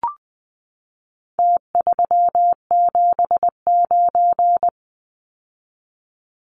Morse code, heard as a keyed tone, about 20 wpm, 700 Hz. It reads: T379
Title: Numbers - Contest Style - 4-Digit 20wpm